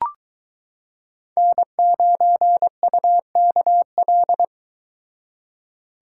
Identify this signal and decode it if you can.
Morse code, heard as a keyed tone, about 22 wpm, 700 Hz. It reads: N9UKL